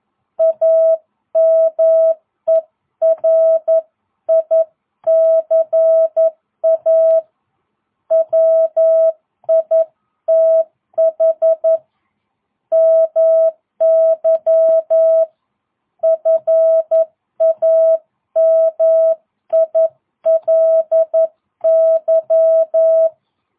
Morse code, heard as a keyed tone, about 11 wpm, 650 Hz. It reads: AMERICA WITH MY FAMILY